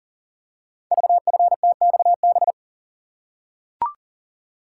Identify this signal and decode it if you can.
Morse code, heard as a keyed tone, about 38 wpm, 700 Hz. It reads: VFTXB
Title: Code Groups - 5 Letters - 40wpm